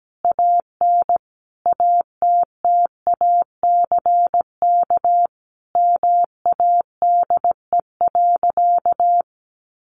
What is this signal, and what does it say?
Morse code, heard as a keyed tone, about 17 wpm, 700 Hz. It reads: AN ATTACK MADE.
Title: Sentences from Top 1500 Words 17wpm